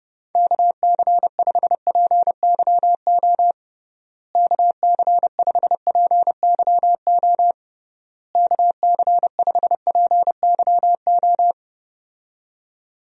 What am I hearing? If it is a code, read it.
Morse code, heard as a keyed tone, about 28 wpm, 700 Hz. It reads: KC5PYO KC5PYO KC5PYO
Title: USA Call Signs - 6 Characters - Repeated 3x - 30wpm